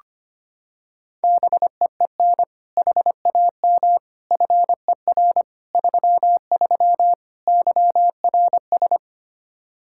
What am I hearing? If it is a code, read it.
Morse code, heard as a keyed tone, about 25 wpm, 700 Hz. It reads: BEEN HAM FER 33 YRS
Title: Practice QSO - Exchange 3 - More Info - 25wpm